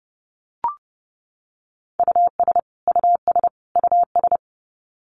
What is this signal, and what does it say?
Morse code, heard as a keyed tone, about 30 wpm, 700 Hz. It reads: US US US
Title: Top 200 Words Repeated 30wpm